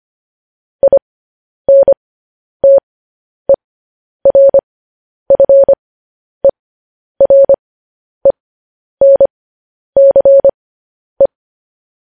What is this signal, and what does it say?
Morse code, heard as a keyed tone, about 25 wpm, spaced as for 10 wpm, 550 Hz. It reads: INTERFERENCE